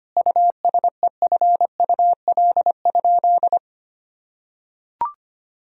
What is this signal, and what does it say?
Morse code, heard as a keyed tone, about 25 wpm, 700 Hz. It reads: USEFUL?
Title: Sentences from Top 1000 Words - Review Part 4 - 25wpm